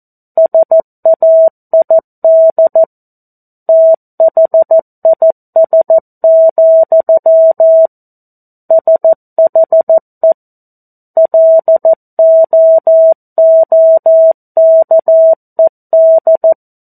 Morse code, heard as a keyed tone, about 14 wpm, 650 Hz. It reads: SAID THIS, SHE LOOKED
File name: 02-Chapter-02 14 WPM